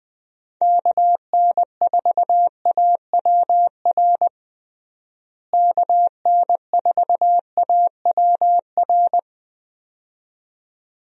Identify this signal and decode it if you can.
Morse code, heard as a keyed tone, about 20 wpm, 700 Hz. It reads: KN4AWR KN4AWR
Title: USA Call Signs - 6 Characters - Repeated 2x - 20wpm